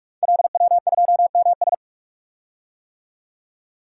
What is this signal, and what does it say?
Morse code, heard as a keyed tone, about 44 wpm, 690 Hz. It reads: LW2MS
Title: Code Groups - 5 Letter:Numbers - 45wpm